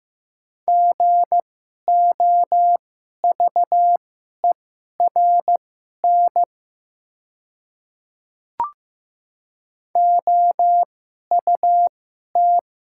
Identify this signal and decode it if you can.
Morse code, heard as a keyed tone, about 15 wpm, spaced as for 11 wpm, 700 Hz. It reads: GOVERN OUT
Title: 2x Character Spacing - Top 500 Words - 15wpm